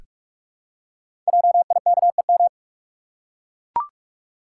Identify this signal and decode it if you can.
Morse code, heard as a keyed tone, about 44 wpm, 690 Hz. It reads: JIKEM
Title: Code Groups - 5 Letters - 45wpm